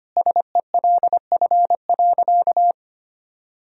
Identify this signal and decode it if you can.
Morse code, heard as a keyed tone, about 25 wpm, 700 Hz. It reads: SELF.
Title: Sentences from Top 1000 Words - Review Part 1 - 25wpm